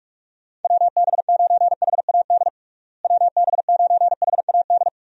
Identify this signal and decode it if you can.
Morse code, heard as a keyed tone, about 46 wpm, 700 Hz. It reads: WB9HAD WB9HAD
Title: USA Call Signs - 6 Characters - Repeated 3x - 45wpm